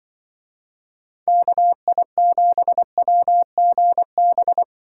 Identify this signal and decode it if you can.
Morse code, heard as a keyed tone, about 23 wpm, 700 Hz. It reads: KI7WGB
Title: Speed Racing - USA Call Signs 20wpm